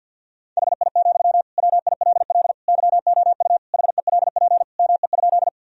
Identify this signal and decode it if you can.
Morse code, heard as a keyed tone, about 53 wpm, 700 Hz. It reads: HI, WILL YOU HELP ME?
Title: Sentences from Top 1000 Words - Review Part 3 - 50wpm